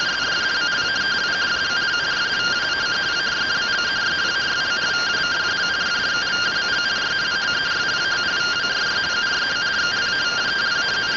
Two-tone teleprinter signal, about 170 Hz shift.